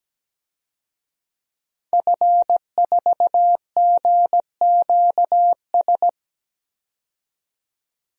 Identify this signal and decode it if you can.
Morse code, heard as a keyed tone, about 17 wpm, 700 Hz. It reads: F4GQS
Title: DXCC-100 Call Signs - 5 Characters - 17wpm